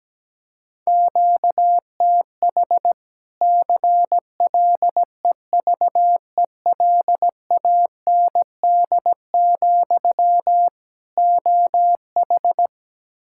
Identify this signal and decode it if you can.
Morse code, heard as a keyed tone, about 17 wpm, 700 Hz. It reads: QTH CLEVELAND, OH
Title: Practice QSO - Exchange 1 - Beginning - 17wpm